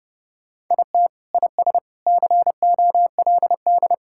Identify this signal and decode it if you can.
Morse code, heard as a keyed tone, about 31 wpm, 700 Hz. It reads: IT IS COLD
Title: Sentences from Top 1000 Words - Review Part 6 - 30wpm